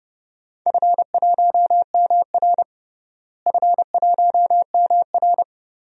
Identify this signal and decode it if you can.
Morse code, heard as a keyed tone, about 30 wpm, 700 Hz. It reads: F1MR F1MR
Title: DXCC-100 Call Signs - 4 Characters - Repeated 2x - 30wpm